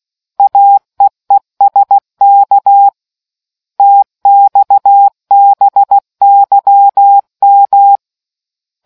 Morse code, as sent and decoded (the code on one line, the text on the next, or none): .- . ... -.- / - -..- -... -.-- --
AEESK TXBYM